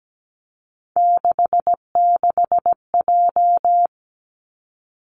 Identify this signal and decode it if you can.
Morse code, heard as a keyed tone, about 17 wpm, 700 Hz. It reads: 66J
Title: Three Challenging Letter-Number Combinations 17wpm